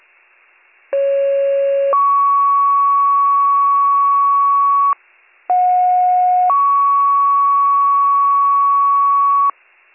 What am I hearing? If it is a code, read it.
Two-tone page, heard as a one-second tone, then a three-second tone, about 550 then 1120 Hz, about 730 then 1120 Hz. It is heard from 0.9 to 9.5 s.